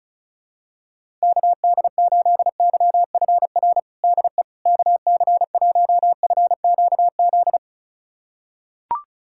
Morse code, heard as a keyed tone, about 36 wpm, 700 Hz. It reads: KD8YFR DE KC1FQZ
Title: USA Callsign DE Callsign 35wpm